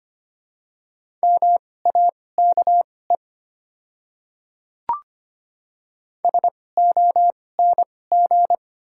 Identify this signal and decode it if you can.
Morse code, heard as a keyed tone, about 25 wpm, spaced as for 18 wpm, 700 Hz. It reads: MAKE SONG
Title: 2x Character Spacing - Top 500 Words - 25wpm